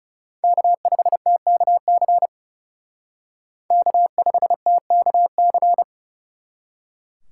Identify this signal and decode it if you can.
Morse code, heard as a keyed tone, about 39 wpm, 700 Hz. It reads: K5TKC K5TKC